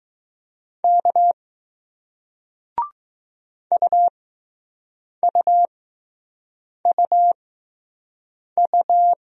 Morse code, heard as a keyed tone, about 20 wpm, spaced as for 6 wpm, 700 Hz. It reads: K UUUU